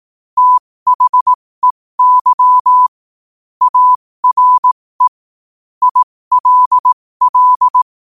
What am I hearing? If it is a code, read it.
Morse code, heard as a keyed tone, about 18 wpm, spaced as for 15 wpm, 1000 Hz. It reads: THEY ARE ILL
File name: qotdfiles / 15wpm